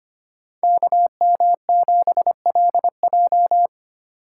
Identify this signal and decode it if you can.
Morse code, heard as a keyed tone, about 27 wpm, 700 Hz. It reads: KM7LJ
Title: Speed Racing - USA 2x2 Call Signs 25wpm